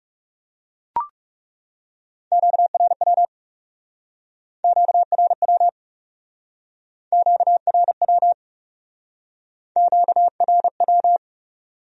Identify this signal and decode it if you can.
Morse code, heard as a keyed tone, about 39 wpm, 700 Hz. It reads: QRW QRW QRW QRW